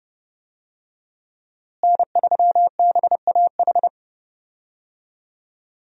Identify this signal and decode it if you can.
Morse code, heard as a keyed tone, about 30 wpm, 700 Hz. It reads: N3BAH
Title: USA Call Signs - 5 Characters - Repeated 2x - 30wpm